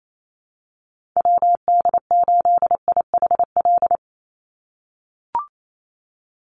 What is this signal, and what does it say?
Morse code, heard as a keyed tone, about 28 wpm, 700 Hz. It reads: WD8IHL
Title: USA Call Signs 28wpm